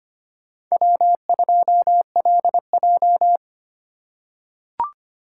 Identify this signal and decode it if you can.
Morse code, heard as a keyed tone, about 25 wpm, 700 Hz. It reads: W2LJ
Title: USA Call Signs - 4 Characters - 2x Character Spacing - 25wpm